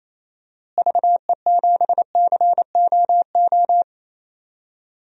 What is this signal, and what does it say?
Morse code, heard as a keyed tone, about 28 wpm, 700 Hz. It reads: VE7COO
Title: DXCC-100 Call Signs - 6 Characters - Repeated 3x - 28wpm